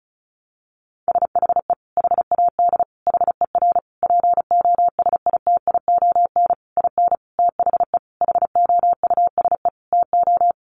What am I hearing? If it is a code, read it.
Morse code, heard as a keyed tone, about 34 wpm, 700 Hz. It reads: SHE HAD HER POSITION IN THE HOUSE TO
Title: Sentences from Top 900 Words 35wpm